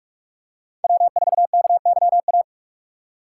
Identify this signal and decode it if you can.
Morse code, heard as a keyed tone, about 45 wpm, 700 Hz. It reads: W4KYA
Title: USA Call Signs - 5 Characters - 4x Character Spacing - 45wpm